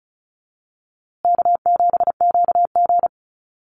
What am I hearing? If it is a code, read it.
Morse code, heard as a keyed tone, about 34 wpm, 700 Hz. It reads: K7QG
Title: USA Call Signs Repeated 3x 35wpm